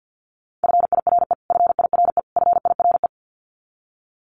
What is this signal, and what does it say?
Morse code, heard as a keyed tone, about 50 wpm, 700 Hz. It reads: FIRE FIRE FIRE
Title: Top 500 Words Repeated 50wpm